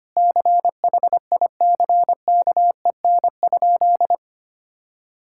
Morse code, heard as a keyed tone, about 25 wpm, 700 Hz. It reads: CHICKEN?
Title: Sentences from Top 1000 Words - Review Part 5 - 25wpm